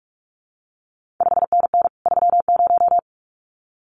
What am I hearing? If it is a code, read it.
Morse code, heard as a keyed tone, about 46 wpm, 700 Hz. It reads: 5NN 30
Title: CQ WW DX Contest Exchange Encore - 45wpm